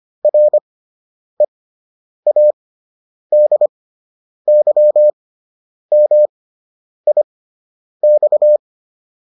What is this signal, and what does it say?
Morse code, heard as a keyed tone, about 25 wpm, spaced as for 9 wpm, 600 Hz. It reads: READYMIX